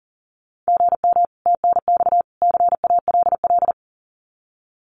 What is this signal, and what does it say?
Morse code, heard as a keyed tone, about 37 wpm, 700 Hz. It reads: GM TNX CALL
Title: Common QSO Phrases 40wpm